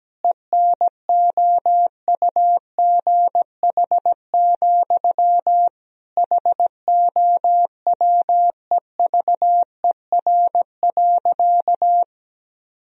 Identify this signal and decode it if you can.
Morse code, heard as a keyed tone, about 17 wpm, 700 Hz. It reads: ENOUGH, HOWEVER.
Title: Sentences from Top 1000 Words - Review Part 1 - 17wpm